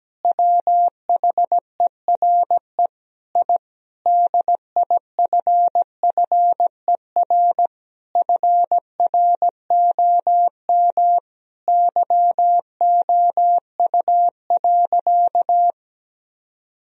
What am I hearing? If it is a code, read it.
Morse code, heard as a keyed tone, about 17 wpm, 700 Hz. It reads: WHERE I DIFFER FROM YOU.
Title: Sentences from Top 1000 Words - Review Part 1 - 17wpm